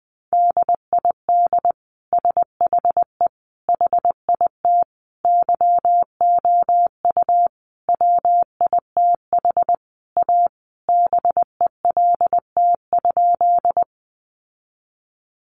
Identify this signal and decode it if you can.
Morse code, heard as a keyed tone, about 20 wpm, 700 Hz. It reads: DID SHE HIT YOU WITH A BELT?
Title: Sentences from Top 1400 Words 20wpm